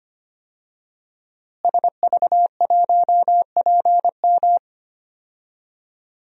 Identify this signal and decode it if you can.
Morse code, heard as a keyed tone, about 25 wpm, 700 Hz. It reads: SV1PM
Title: DXCC-100 Call Signs - 5 Characters - Repeated 3x - 25wpm